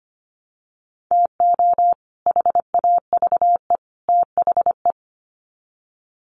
Morse code, heard as a keyed tone, about 25 wpm, 700 Hz. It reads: TO HAVE THE
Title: Sets of 3 Words 25wpm